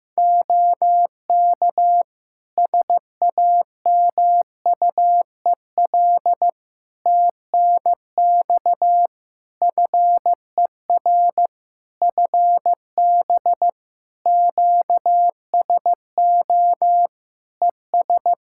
Morse code, heard as a keyed tone, about 15 wpm, 700 Hz. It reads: OK SAMUEL TNX FER FB QSO ES
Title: K1USN SST QSO Practice 15wpm